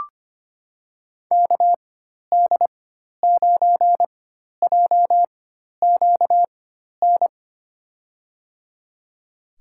Morse code, heard as a keyed tone, about 25 wpm, spaced as for 12 wpm, 700 Hz. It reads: KD9JQN
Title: USA Call Signs - 6 Characters - 4x Character Spacing - 25wpm